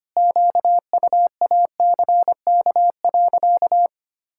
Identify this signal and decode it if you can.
Morse code, heard as a keyed tone, about 25 wpm, 700 Hz. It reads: QUACK.
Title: Pangram Challenge v2 - 25wpm